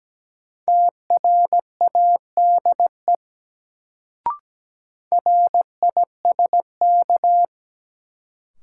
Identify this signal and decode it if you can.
Morse code, heard as a keyed tone, about 17 wpm, 700 Hz. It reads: TRADE RISK
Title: Sentences from Top 600 Words - New Word Drill - 17wpm